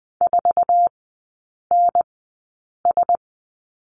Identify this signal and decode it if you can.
Morse code, heard as a keyed tone, about 20 wpm, spaced as for 8 wpm, 700 Hz. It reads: 4NS
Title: 053 Lesson 10e - Letter r - Three Characters 20wpm